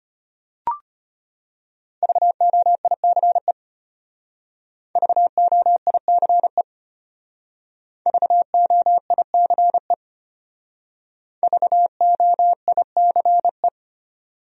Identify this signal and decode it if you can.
Morse code, heard as a keyed tone, about 32 wpm, 700 Hz. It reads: VOICE VOICE VOICE VOICE